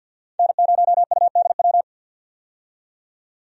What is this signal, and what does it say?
Morse code, heard as a keyed tone, about 48 wpm, 700 Hz. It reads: N0UDW
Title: USA Call Signs - 5 Characters - 50wpm